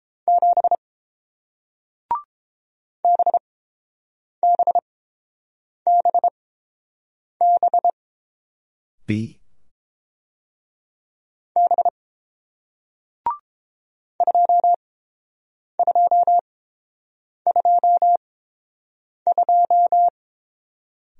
Morse code, heard as a keyed tone, about 31 wpm, spaced as for 7 wpm, 700 Hz. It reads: 7 BBBB B 2222